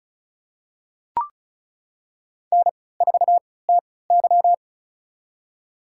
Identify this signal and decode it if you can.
Morse code, heard as a keyed tone, about 34 wpm, spaced as for 20 wpm, 700 Hz. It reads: N4TY
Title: USA Call Signs - 4 Characters - 3x Character Spacing - 35wpm